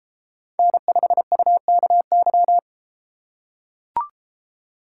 Morse code, heard as a keyed tone, about 31 wpm, 700 Hz. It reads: N5UKY